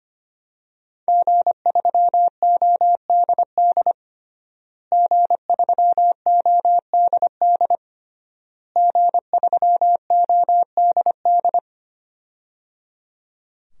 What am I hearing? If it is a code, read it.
Morse code, heard as a keyed tone, about 25 wpm, 700 Hz. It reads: G3ODD G3ODD G3ODD
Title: DXCC-100 Call Signs - 5 Characters - Repeated 3x - 25wpm